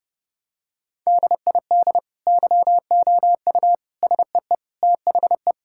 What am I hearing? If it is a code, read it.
Morse code, heard as a keyed tone, about 30 wpm, 700 Hz. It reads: DID YOU SEE THE
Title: Sentences from Top 1000 Words - Review Part 5 - 30wpm